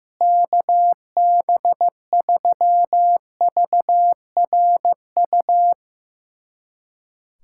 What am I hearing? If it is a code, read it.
Morse code, heard as a keyed tone, about 15 wpm, 700 Hz. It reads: KB3VRU
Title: USA Callsign DE Callsign - 6 Characters - 15wpm